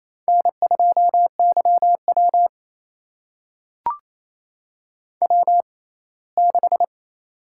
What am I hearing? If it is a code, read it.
Morse code, heard as a keyed tone, about 28 wpm, spaced as for 19 wpm, 700 Hz. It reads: N2YW W6
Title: USA Call Signs - 4 Characters - 6x Character Spacing - 28wpm